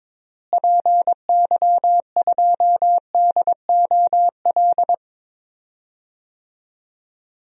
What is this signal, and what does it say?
Morse code, heard as a keyed tone, about 22 wpm, 700 Hz. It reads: PY2DOL
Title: DXCC-100 Call Signs - 6 Characters - Repeated 2x - 22wpm